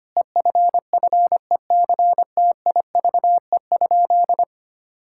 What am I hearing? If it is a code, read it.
Morse code, heard as a keyed tone, about 25 wpm, 700 Hz. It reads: EFFECTIVE?
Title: Sentences from Top 1000 Words - Review Part 6 - 25wpm